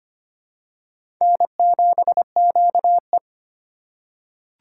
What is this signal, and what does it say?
Morse code, heard as a keyed tone, about 25 wpm, 700 Hz. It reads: N7QE
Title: USA Call Signs - 4 Characters - 2x Character Spacing - 25wpm